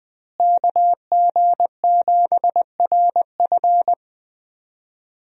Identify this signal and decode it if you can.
Morse code, heard as a keyed tone, about 21 wpm, 700 Hz. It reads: KG7RF
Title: USA Call Signs - 5 Characters - Repeated 3x - 20wpm